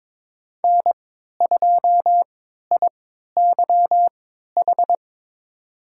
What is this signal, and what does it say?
Morse code, heard as a keyed tone, about 22 wpm, spaced as for 12 wpm, 700 Hz. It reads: N2IYH
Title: USA Call Signs - 5 Characters - 3x Character Spacing - 22wpm